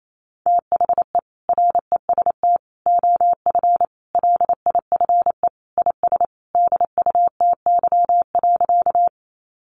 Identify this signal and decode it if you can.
Morse code, heard as a keyed tone, about 28 wpm, 700 Hz. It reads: THE REST OF LIFE IS DUTY.
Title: Sentences from Top 1500 Words 28wpm